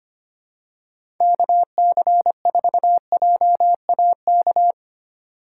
Morse code, heard as a keyed tone, about 25 wpm, 700 Hz. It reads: KC4JAK
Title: USA Call Signs - 4x Character Spacing - 25wpm